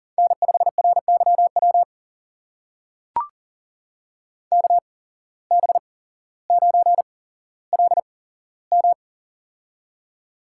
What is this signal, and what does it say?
Morse code, heard as a keyed tone, about 40 wpm, spaced as for 17 wpm, 700 Hz. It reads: N5RYW KB9LM
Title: USA Call Signs - 5 Characters - 8x Character Spacing - 40wpm